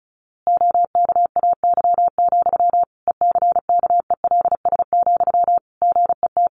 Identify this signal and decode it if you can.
Morse code, heard as a keyed tone, about 34 wpm, 700 Hz. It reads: OKAY, ECKELS, GET
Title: A Sound Of Thunder 35wpm